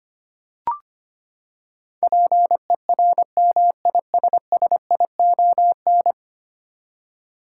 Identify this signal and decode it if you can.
Morse code, heard as a keyed tone, about 25 wpm, 700 Hz. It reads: PERMISSION
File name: Ten Letter Words - 25wpm